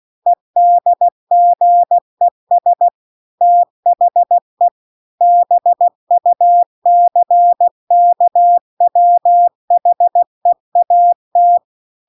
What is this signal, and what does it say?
Morse code, heard as a keyed tone, about 16 wpm, 700 Hz. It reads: EDGES THE BUCKWHEAT